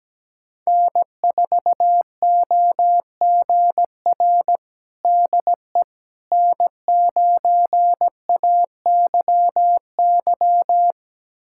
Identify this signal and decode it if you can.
Morse code, heard as a keyed tone, about 17 wpm, 700 Hz. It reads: N4OGR DE N9AYY
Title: USA Callsign DE Callsign - 5 Characters - 17wpm